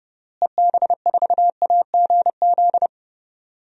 Morse code, heard as a keyed tone, about 30 wpm, 700 Hz. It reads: EB4AGZ
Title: DXCC-100 Call Signs - 6 Characters - Repeated 3x - 30wpm